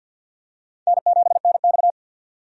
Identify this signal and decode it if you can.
Morse code, heard as a keyed tone, about 53 wpm, 700 Hz. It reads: N7NX
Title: USA Call Signs - 4 Characters - 50wpm